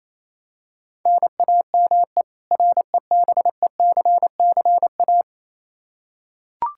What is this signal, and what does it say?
Morse code, heard as a keyed tone, about 28 wpm, 700 Hz. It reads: NAME REBECCA
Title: K1USN SST QSO Practice 28wpm